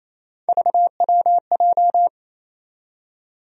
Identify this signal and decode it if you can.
Morse code, heard as a keyed tone, about 28 wpm, 700 Hz. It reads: VWJ
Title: USA Call Sign Postfixes - 3 Letters 28wpm